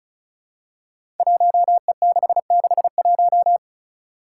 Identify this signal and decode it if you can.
Morse code, heard as a keyed tone, about 36 wpm, 700 Hz. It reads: 1E661
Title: Five Challenging Letter-Number Combinations 35wpm